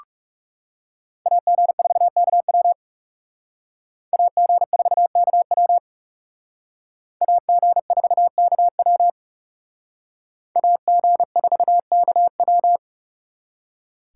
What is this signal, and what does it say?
Morse code, heard as a keyed tone, about 39 wpm, 700 Hz. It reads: AG4KW AG4KW AG4KW AG4KW